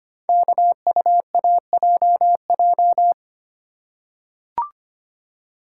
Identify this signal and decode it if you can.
Morse code, heard as a keyed tone, about 25 wpm, 700 Hz. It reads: KUAJJ
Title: Code Groups - 5 Letters - 25wpm